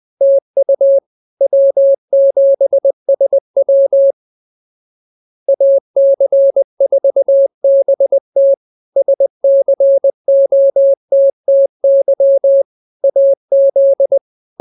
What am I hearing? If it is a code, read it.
Morse code, heard as a keyed tone, about 20 wpm, 550 Hz. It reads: TU W7SW AC4BT SCOTTY AZ